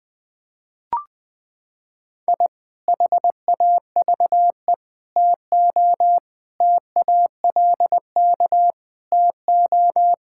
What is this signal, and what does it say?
Morse code, heard as a keyed tone, about 21 wpm, 700 Hz. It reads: I HAVE TO TALK TO
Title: Sentences with 3-Word Phrases - 20wpm